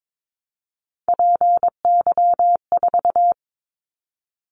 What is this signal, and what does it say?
Morse code, heard as a keyed tone, about 22 wpm, 700 Hz. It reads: PY4
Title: Callsign Prefixes for Top 100 DXCC - 22wpm